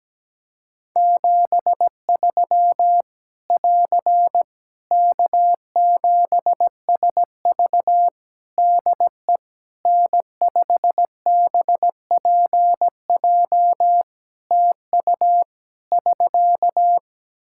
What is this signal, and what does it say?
Morse code, heard as a keyed tone, about 17 wpm, 700 Hz. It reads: 73 <AR> K7SV DE N5BPJ TU <SK>
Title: Practice QSO - Exchange 4 - Ending - 17wpm